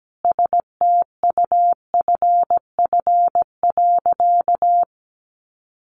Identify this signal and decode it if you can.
Morse code, heard as a keyed tone, about 17 wpm, 700 Hz. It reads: STUFF.
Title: Sentences from Top 1400 Words 17wpm